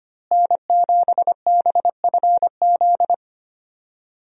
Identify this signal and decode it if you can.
Morse code, heard as a keyed tone, about 25 wpm, 700 Hz. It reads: N7BFZ
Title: USA Call Signs - 5 Characters - Repeated 2x - 25wpm